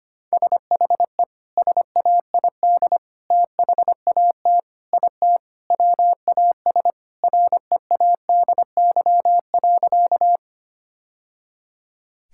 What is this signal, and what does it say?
Morse code, heard as a keyed tone, about 25 wpm, 700 Hz. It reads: SHE SAID THAT IT WAS READY.
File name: practice / 25wpm / Sentences with 3-Word Phrases - 25wpm